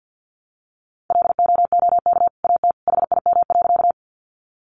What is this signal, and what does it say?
Morse code, heard as a keyed tone, about 50 wpm, 700 Hz. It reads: LOOK AT HIM.
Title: Sentences from Top 100 Words 50wpm